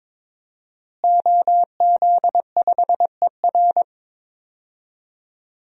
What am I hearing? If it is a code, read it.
Morse code, heard as a keyed tone, about 22 wpm, 700 Hz. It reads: OZ5ER
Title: DXCC-100 Call Signs - 5 Characters - Repeated 3x - 22wpm